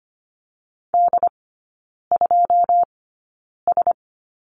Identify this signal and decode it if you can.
Morse code, heard as a keyed tone, about 25 wpm, spaced as for 9 wpm, 700 Hz. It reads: D2S
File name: Lesson 17e - Letter u - Three Characters 25wpm